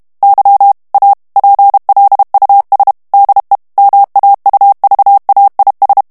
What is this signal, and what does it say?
Morse code, heard as a keyed tone, about 32 wpm, 800 Hz. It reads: Y A PLUS DE MAUVAIS
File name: morse mp3 / FRANCAIS TEXTE CLAIR 1 VITESSE 30 MOTS MINUTES